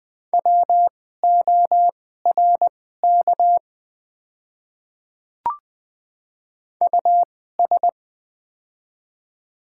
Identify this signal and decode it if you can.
Morse code, heard as a keyed tone, about 20 wpm, spaced as for 15 wpm, 700 Hz. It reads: WORK US
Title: 2x Character Spacing - Top 200 Words - 20wpm